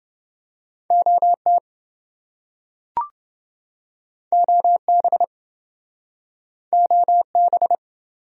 Morse code, heard as a keyed tone, about 28 wpm, 700 Hz. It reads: OT OB OB